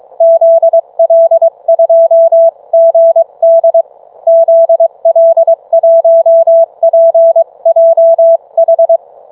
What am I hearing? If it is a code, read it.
Morse code, heard as a keyed tone, about 22 wpm, 660 Hz. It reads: ZL2GD ZL1PJH